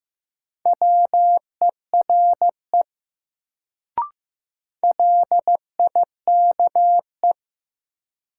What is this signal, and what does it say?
Morse code, heard as a keyed tone, about 15 wpm, 700 Hz. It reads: WERE LIKE
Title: Sentences from Top 100 Words - New Word Drill - 15wpm